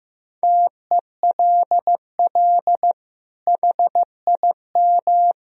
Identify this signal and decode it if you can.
Morse code, heard as a keyed tone, about 15 wpm, 700 Hz. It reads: TELL HIM
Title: Sentences from Top 1000 Words - Review Part 2 - 15wpm